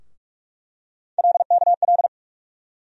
Morse code, heard as a keyed tone, about 46 wpm, 700 Hz. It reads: LKL
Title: USA Call Sign Postfixes - 3 Letters 45wpm